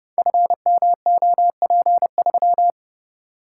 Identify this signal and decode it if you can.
Morse code, heard as a keyed tone, about 30 wpm, 700 Hz. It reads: FMOP3
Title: Code Groups - 5 Letter:Numbers - 30wpm